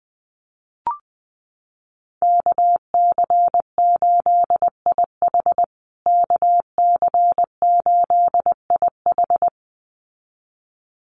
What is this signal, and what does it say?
Morse code, heard as a keyed tone, about 20 wpm, 700 Hz. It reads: KC8IH KC8IH
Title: USA Call Signs Repeated 2x 20wpm